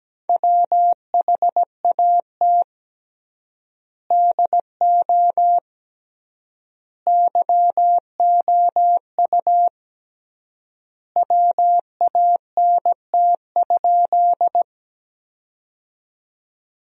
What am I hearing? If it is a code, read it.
Morse code, heard as a keyed tone, about 17 wpm, 700 Hz. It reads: WHAT DO YOU WANT?
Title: 3x Word-Spacing - Sentences from Top 200 Words - 17wpm